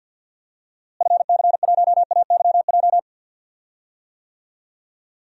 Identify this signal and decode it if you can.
Morse code, heard as a keyed tone, about 46 wpm, 700 Hz. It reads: FC1AYJ
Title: DXCC-100 Call Signs - 6 Characters - Repeated 3x - 50wpm